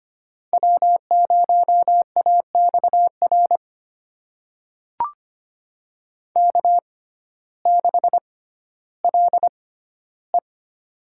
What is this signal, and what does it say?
Morse code, heard as a keyed tone, about 25 wpm, spaced as for 14 wpm, 700 Hz. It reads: W0AXR K6LE
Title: USA Call Signs - 6x Character Spacing - 25wpm